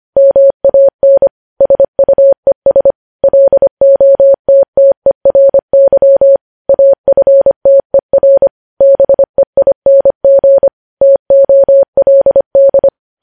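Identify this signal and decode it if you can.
Morse code, heard as a keyed tone, about 25 wpm, 550 Hz. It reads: MAN SUES LOTTERY AFTER BEING TOLD
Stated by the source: Morse code news headlines sent at 25 wpm.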